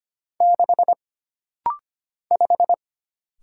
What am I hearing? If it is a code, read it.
Morse code, heard as a keyed tone, about 25 wpm, 700 Hz. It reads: 6 5
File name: Single Character (Letter:Number:Punctuation:Prosign) - Rapid-Fire - 25wpm